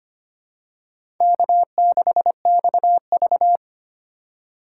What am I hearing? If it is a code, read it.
Morse code, heard as a keyed tone, about 25 wpm, 700 Hz. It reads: K6XV
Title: USA Call Signs - 4 Characters - 2x Character Spacing - 25wpm